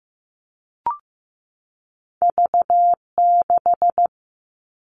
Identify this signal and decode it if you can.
Morse code, heard as a keyed tone, about 15 wpm, 700 Hz. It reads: V6
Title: All Cut Numbers - 2-Digit 15wpm